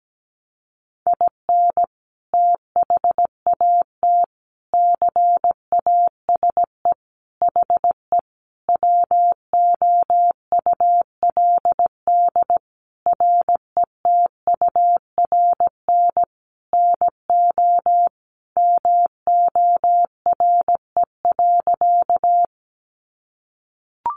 Morse code, heard as a keyed tone, about 17 wpm, 700 Hz. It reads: IN THAT CASE HE WOULD RETURN NO MORE.
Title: Sentences from Top 800 Words 17wpm